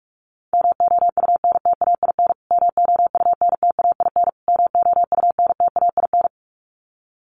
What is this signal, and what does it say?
Morse code, heard as a keyed tone, about 44 wpm, 700 Hz. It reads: MOUNTAIN MOUNTAIN MOUNTAIN
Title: Top 500 Words Repeated 45wpm